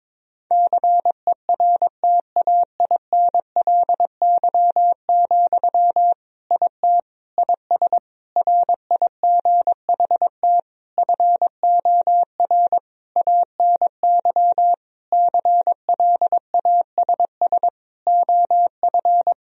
Morse code, heard as a keyed tone, about 22 wpm, 700 Hz. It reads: CERTAINLY, IT IS RIGHT FOR ANY CLASS OF
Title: Sentences from Top 1000 Words - Review Part 2 - 22wpm